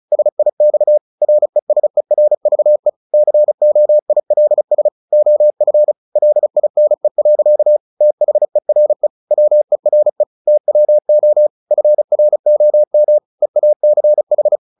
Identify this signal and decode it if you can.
Morse code, heard as a keyed tone, about 34 wpm, 600 Hz. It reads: SIX RESERVE COILS OF LINE. THERE WERE TWO FROM EACH
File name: the old man and the sea / the old man and the sea 1 35wpm